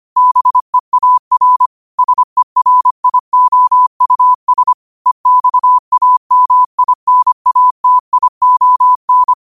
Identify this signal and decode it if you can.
Morse code, heard as a keyed tone, about 25 wpm, 1000 Hz. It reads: BEAR SERIOUS EXAMINATION
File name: qotdfiles / 25wpm